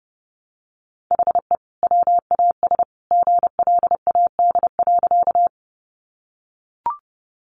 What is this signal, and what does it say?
Morse code, heard as a keyed tone, about 28 wpm, 700 Hz. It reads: HE WAS GLAD.
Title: Sentences from Top 800 Words 30wpm